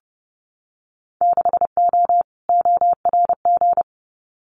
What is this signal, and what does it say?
Morse code, heard as a keyed tone, about 30 wpm, 700 Hz. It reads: 6O ORG
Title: Winter Field Day Exchanges Encore 30wpm